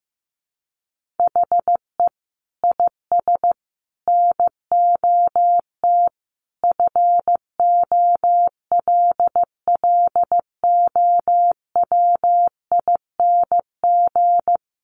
Morse code, heard as a keyed tone, about 15 wpm, 700 Hz. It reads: HE IS NOT FOLLOWING
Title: Sentences from Top 700 Words 15wpm